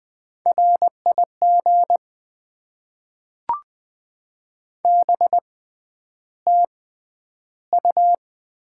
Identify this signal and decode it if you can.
Morse code, heard as a keyed tone, about 20 wpm, spaced as for 10 wpm, 700 Hz. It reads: RIG BTU
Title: Top 20 QSO Elements - 6x Character Spacing - 20wpm